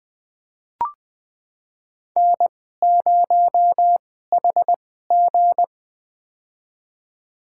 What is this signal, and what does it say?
Morse code, heard as a keyed tone, about 20 wpm, spaced as for 15 wpm, 700 Hz. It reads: N0HG
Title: USA Call Signs - 4 Characters - 2x Character Spacing - 20wpm